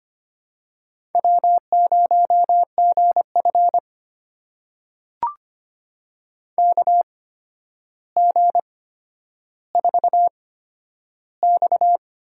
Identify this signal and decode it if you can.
Morse code, heard as a keyed tone, about 25 wpm, spaced as for 11 wpm, 700 Hz. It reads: W0GF KG4X
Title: USA Call Signs - 4 Characters - 8x Character Spacing - 25wpm